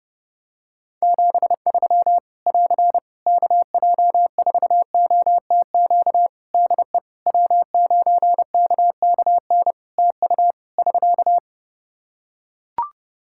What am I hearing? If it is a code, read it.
Morse code, heard as a keyed tone, about 30 wpm, 700 Hz. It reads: 73 <AR> KJ4OTQ DE W9KKN TU <SK>
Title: Practice QSO - Exchange 4 - Ending - 30wpm